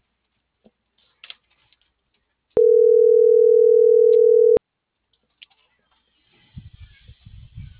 Ringback tone (North America), 440 + 480 Hz.